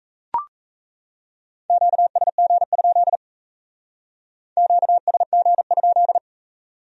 Morse code, heard as a keyed tone, about 35 wpm, 700 Hz. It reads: QSG? QSG?